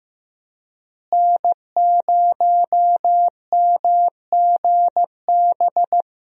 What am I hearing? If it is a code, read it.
Morse code, heard as a keyed tone, about 15 wpm, 700 Hz. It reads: N0MGB
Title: USA Call Signs - 5 Characters - Repeated 3x - 15wpm